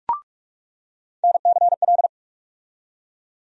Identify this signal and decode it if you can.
Morse code, heard as a keyed tone, about 46 wpm, 700 Hz. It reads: NCL
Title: USA Call Sign Postfixes - 3 Letters 45wpm